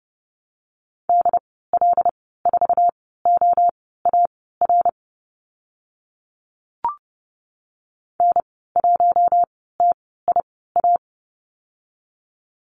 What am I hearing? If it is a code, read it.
Morse code, heard as a keyed tone, about 30 wpm, spaced as for 17 wpm, 700 Hz. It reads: DL4OAR N1TIA
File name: Lesson 14g - Number 2 - Call Signs 30wpm